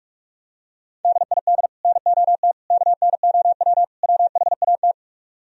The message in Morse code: DID NOT KNOW WHAT

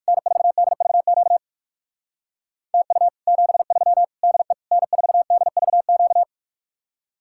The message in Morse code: N4DVQ TU 73 DE N4DVQ

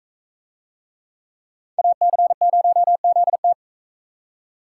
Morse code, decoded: AC0ZT

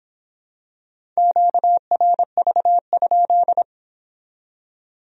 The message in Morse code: QRV?

